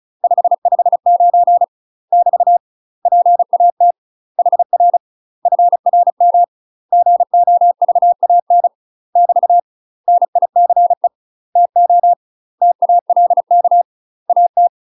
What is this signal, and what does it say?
Morse code, heard as a keyed tone, about 35 wpm, 700 Hz. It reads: 559 <BT> PAT HR FRM GOVAN <BT> NICE TO TALK AT
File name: UK MASTER QSOs TRAINING / 35WPM / UK MASTER QSOs 0014